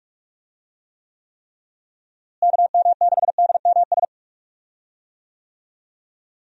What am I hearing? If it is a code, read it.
Morse code, heard as a keyed tone, about 45 wpm, 700 Hz. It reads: KM6DMS
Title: USA Call Signs - 6 Characters - Repeated 3x - 45wpm